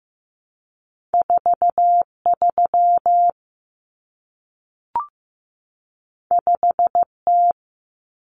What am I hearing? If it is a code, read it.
Morse code, heard as a keyed tone, about 15 wpm, 700 Hz. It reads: 43 5T
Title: Common Cut Numbers - 2-Digit 15wpm